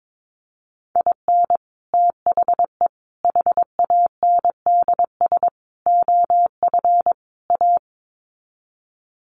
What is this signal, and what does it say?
Morse code, heard as a keyed tone, about 22 wpm, 700 Hz. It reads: IN THE HANDS OF A
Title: Sets of 5 Words Encore 22wpm